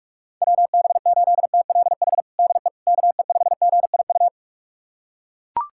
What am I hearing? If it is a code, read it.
Morse code, heard as a keyed tone, about 41 wpm, 700 Hz. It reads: WB8TLH DE KE5GIU
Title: USA Callsign DE Callsign - 6 Characters - 45wpm